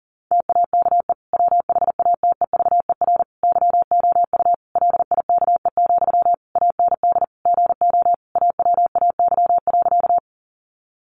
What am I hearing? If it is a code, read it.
Morse code, heard as a keyed tone, about 40 wpm, 700 Hz. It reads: TAKE WHATEVER YOU LIKE, AND GO AWAY.
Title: Sentences from Top 1100 Words 40wpm